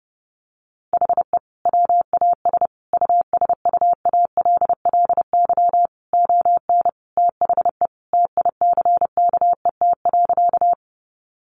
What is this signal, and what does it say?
Morse code, heard as a keyed tone, about 30 wpm, 700 Hz. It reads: HE WAS USUALLY ON THE TICKET.
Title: Sentences from Top 1400 Words 30wpm